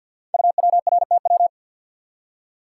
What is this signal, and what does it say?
Morse code, heard as a keyed tone, about 50 wpm, 690 Hz. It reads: UWRTW
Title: Code Groups - 5 Letters - 50wpm